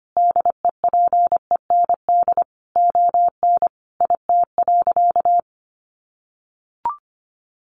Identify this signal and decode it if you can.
Morse code, heard as a keyed tone, about 25 wpm, 700 Hz. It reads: DEPEND ON IT.
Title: Sentences from Top 1300 Words 25wpm